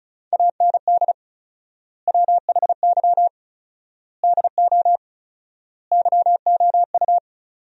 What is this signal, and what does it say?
Morse code, heard as a keyed tone, about 36 wpm, 700 Hz. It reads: AND WHY DO YOU
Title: 4x Word-Spacing - Sentences from Top 400 Words - 35wpm